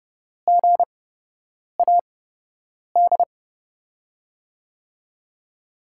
Morse code, heard as a keyed tone, about 30 wpm, spaced as for 8 wpm, 700 Hz. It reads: GAD
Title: USA Call Signs - 5 Characters - 8x Character Spacing - 30wpm